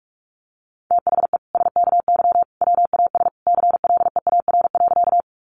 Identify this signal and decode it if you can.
Morse code, heard as a keyed tone, about 44 wpm, 700 Hz. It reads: THE SKY WAS CLEAR.
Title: Sentences from Top 1200 Words 45wpm